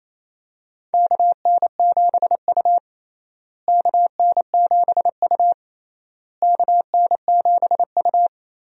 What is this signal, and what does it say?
Morse code, heard as a keyed tone, about 28 wpm, 700 Hz. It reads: KN7U KN7U KN7U
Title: USA Call Signs - 4 Characters - Repeated 3x - 28wpm